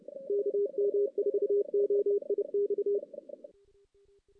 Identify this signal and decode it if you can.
Morse code, heard as a keyed tone, about 30 wpm, 410 Hz. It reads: KM4OIX
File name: 2016 JAN CW NAQP / 2016 Jan naqp k5ot - 0-5